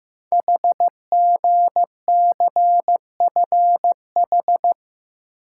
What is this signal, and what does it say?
Morse code, heard as a keyed tone, about 15 wpm, 700 Hz. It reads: HGCFH
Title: Code Groups - 5 Letters - 15wpm